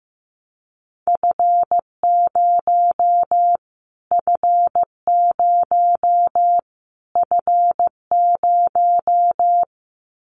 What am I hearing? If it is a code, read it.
Morse code, heard as a keyed tone, about 15 wpm, 700 Hz. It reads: F0 F0 F0
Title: Callsign Prefixes for Top 25 DXCC - Repeated 3x - 15wpm